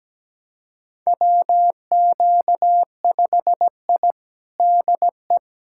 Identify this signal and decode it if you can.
Morse code, heard as a keyed tone, about 17 wpm, 700 Hz. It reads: WQ5I DE
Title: USA Callsign DE Callsign - 4 Characters - 17wpm